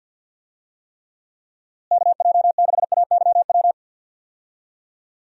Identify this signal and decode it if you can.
Morse code, heard as a keyed tone, about 51 wpm, 700 Hz. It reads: KJ6AYW
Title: USA Call Signs - 6 Characters - Repeated 3x - 50wpm